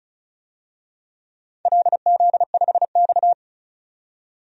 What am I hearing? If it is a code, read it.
Morse code, heard as a keyed tone, about 36 wpm, 690 Hz. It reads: LZ5X
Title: DXCC-100 Call Signs - 4 Characters - 35wpm